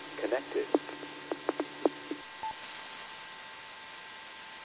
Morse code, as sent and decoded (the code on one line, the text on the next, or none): none